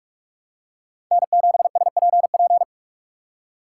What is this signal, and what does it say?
Morse code, heard as a keyed tone, about 46 wpm, 700 Hz. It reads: N7SPP
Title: USA Call Signs - 5 Characters - 2x Character Spacing - 45wpm